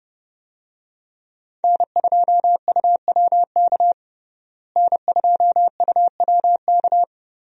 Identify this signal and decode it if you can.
Morse code, heard as a keyed tone, about 30 wpm, 700 Hz. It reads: N2UWK N2UWK
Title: USA Call Signs - 5 Characters - Repeated 3x - 30wpm